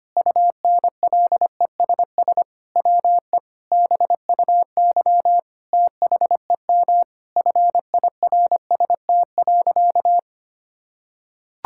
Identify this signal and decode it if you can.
Morse code, heard as a keyed tone, about 25 wpm, 700 Hz. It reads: UNLESS WE BUY THEM FIRST.
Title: Sentences from Top 1000 Words - Review Part 6 - 25wpm